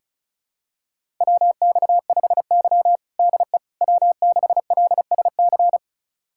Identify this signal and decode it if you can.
Morse code, heard as a keyed tone, about 36 wpm, 700 Hz. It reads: WX5Y DE W6LSC
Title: USA Callsign DE Callsign 35wpm